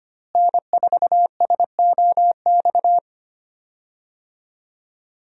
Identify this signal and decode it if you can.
Morse code, heard as a keyed tone, about 25 wpm, 700 Hz. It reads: N4SOX
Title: USA Call Signs - 5 Characters - Repeated 3x - 25wpm